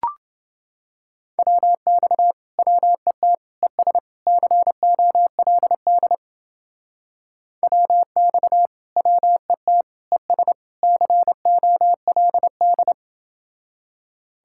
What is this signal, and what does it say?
Morse code, heard as a keyed tone, about 30 wpm, 700 Hz. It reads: WX WET ES COLD WX WET ES COLD